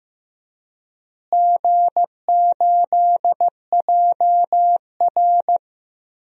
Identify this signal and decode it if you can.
Morse code, heard as a keyed tone, about 15 wpm, 700 Hz. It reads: G8JR
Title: DXCC-100 Call Signs - 4 Characters - Repeated 2x - 15wpm